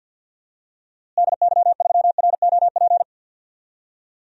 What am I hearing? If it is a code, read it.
Morse code, heard as a keyed tone, about 53 wpm, 700 Hz. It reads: DY3ROP
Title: DXCC-100 Call Signs - 6 Characters - Repeated 2x - 50wpm